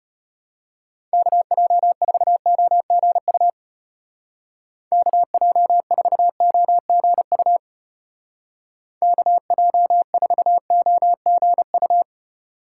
Morse code, heard as a keyed tone, about 32 wpm, 700 Hz. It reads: KJ4OGU KJ4OGU KJ4OGU